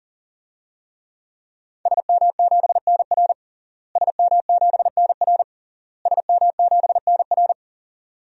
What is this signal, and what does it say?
Morse code, heard as a keyed tone, about 40 wpm, 700 Hz. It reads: SM7NR SM7NR SM7NR
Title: DXCC-100 Call Signs - 5 Characters - Repeated 3x - 40wpm